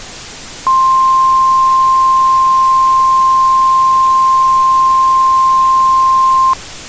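Emergency Alert System (SAME) tones (the 1050 Hz attention tone). It is heard from 0.6 to 6.6 s.
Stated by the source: Emergency alert tones used in a game map.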